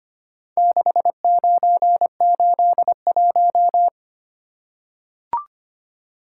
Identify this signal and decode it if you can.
Morse code, heard as a keyed tone, about 25 wpm, 700 Hz. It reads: 6981
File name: practice / 25wpm / Four Letter-Number Combinations - 25wpm